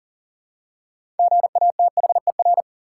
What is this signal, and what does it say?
Morse code, heard as a keyed tone, about 40 wpm, 700 Hz. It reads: GATHER
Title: Six Letter Words - 40wpm